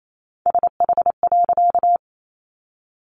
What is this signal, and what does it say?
Morse code, heard as a keyed tone, about 28 wpm, 700 Hz. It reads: SH.
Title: A Sound Of Thunder 28wpm